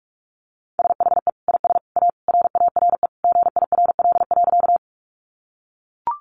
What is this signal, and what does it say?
Morse code, heard as a keyed tone, about 44 wpm, 700 Hz. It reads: SHE IS A RARE GIRL.
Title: Sentences from Top 1100 Words 45wpm